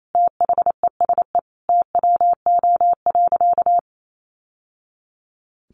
Morse code, heard as a keyed tone, about 28 wpm, 700 Hz. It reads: THESE TWO.
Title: Sentences from Top 1200 Words 28wpm